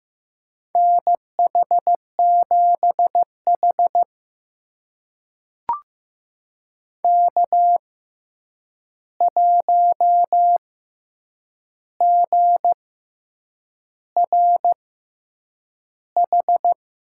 Morse code, heard as a keyed tone, about 15 wpm, spaced as for 7 wpm, 700 Hz. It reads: NH7H K1GRH
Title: USA Call Signs - 6x Character Spacing - 15wpm